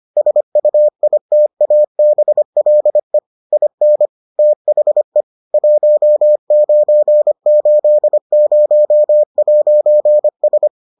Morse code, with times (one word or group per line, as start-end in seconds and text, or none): 0.2-3.2 SUITABLE
3.5-4.1 IN
4.4-5.2 THE
5.5-10.7 1980'S